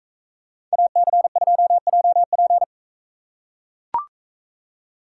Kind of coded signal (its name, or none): Morse code